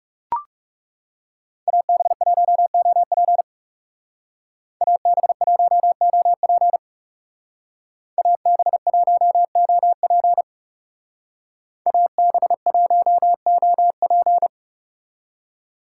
Morse code, heard as a keyed tone, about 37 wpm, 700 Hz. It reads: AB1OP AB1OP AB1OP AB1OP
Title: Speed Racing - USA 2x2 Call Signs 30wpm